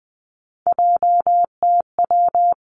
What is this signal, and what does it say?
Morse code, heard as a keyed tone, about 20 wpm, 700 Hz. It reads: JTW
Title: Three Challenging Letter-Number Combinations 20wpm